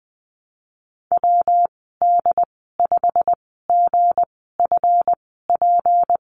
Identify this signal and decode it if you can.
Morse code, heard as a keyed tone, about 20 wpm, spaced as for 15 wpm, 700 Hz. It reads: WD5GFP
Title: 167 Lesson 27g - Number 7 - Call Signs 20wpm